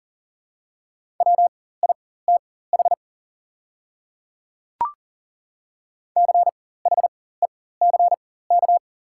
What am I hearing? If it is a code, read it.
Morse code, heard as a keyed tone, about 40 wpm, spaced as for 19 wpm, 700 Hz. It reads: WITH CHECK
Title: 4x Character Spacing - Top 500 Words - 40wpm